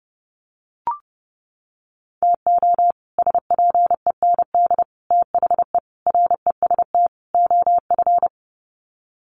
Morse code, heard as a keyed tone, about 30 wpm, 700 Hz. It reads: TO SPEND THE REST OF
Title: Sets of 5 Words Encore2 30wpm